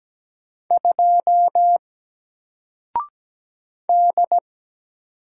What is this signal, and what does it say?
Morse code, heard as a keyed tone, about 17 wpm, 700 Hz. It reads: 2 D
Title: Single Letter-Number Alt 17wpm